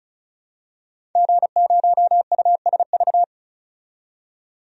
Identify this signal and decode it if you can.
Morse code, heard as a keyed tone, about 36 wpm, 700 Hz. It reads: G0USV
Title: DXCC-100 Call Signs - 5 Characters - 35wpm